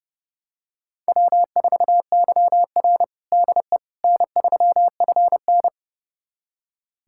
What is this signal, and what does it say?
Morse code, heard as a keyed tone, about 30 wpm, 700 Hz. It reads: W4YR DE N3FN
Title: USA Callsign DE Callsign - 4 Characters - 30wpm